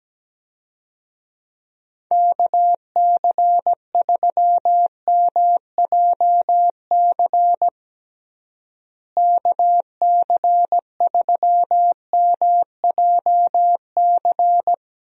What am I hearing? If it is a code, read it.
Morse code, heard as a keyed tone, about 17 wpm, 700 Hz. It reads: KC3MJC KC3MJC
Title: USA Call Signs - 6 Characters - Repeated 3x - 17wpm